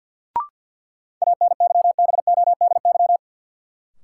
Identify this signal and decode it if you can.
Morse code, heard as a keyed tone, about 53 wpm, 700 Hz. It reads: ANYBODY
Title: Sets of 5 Words - New Word Drill - 50wpm